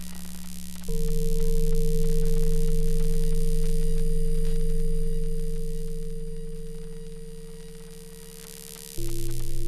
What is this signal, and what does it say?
Dial tone (China, 450 Hz).